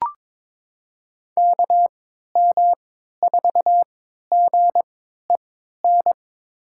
Morse code, heard as a keyed tone, about 22 wpm, spaced as for 13 wpm, 700 Hz. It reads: KM4GEN
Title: USA Call Signs - 3x Character Spacing - 22wpm